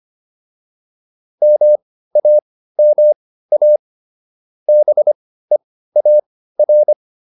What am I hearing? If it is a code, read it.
Morse code, heard as a keyed tone, about 25 wpm, spaced as for 15 wpm, 600 Hz. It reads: MAMA BEAR